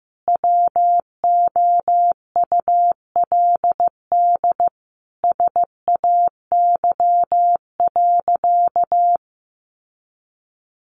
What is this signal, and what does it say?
Morse code, heard as a keyed tone, about 15 wpm, 700 Hz. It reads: WOULD SAY.
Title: Sentences from Top 1500 Words 15wpm